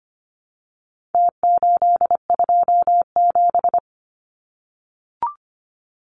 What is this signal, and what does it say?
Morse code, heard as a keyed tone, about 25 wpm, 700 Hz. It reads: T827
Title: Numbers - Contest Style - 4-Digit 25wpm